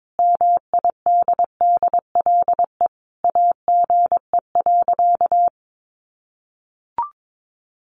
Morse code, heard as a keyed tone, about 22 wpm, 700 Hz. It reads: MIDDLE AGE.
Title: Sentences from Top 1400 Words 22wpm